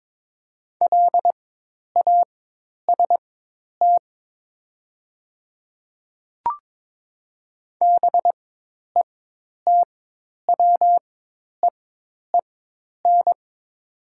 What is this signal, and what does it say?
Morse code, heard as a keyed tone, about 22 wpm, spaced as for 10 wpm, 700 Hz. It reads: LAST BETWEEN
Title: 4x Character Spacing - Top 300 Words - 22wpm